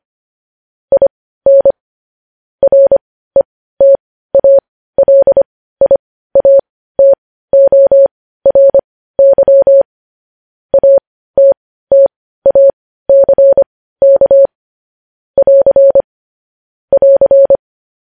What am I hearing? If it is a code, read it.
Morse code, heard as a keyed tone, about 25 wpm, spaced as for 15 wpm, 550 Hz. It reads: IN RETALIATORY ATTACK <AR> <AR>